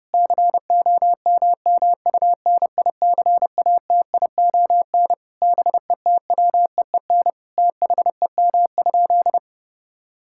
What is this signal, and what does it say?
Morse code, heard as a keyed tone, about 30 wpm, 700 Hz. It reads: COMMUNICATION BETWEEN THEM?
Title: Sentences from Top 1000 Words - Review Part 6 - 30wpm